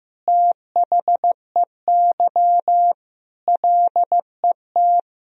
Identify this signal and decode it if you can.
Morse code, heard as a keyed tone, about 15 wpm, 700 Hz. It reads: THEY LET
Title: Sentences from Top 400 Words and Abbreviations 15wpm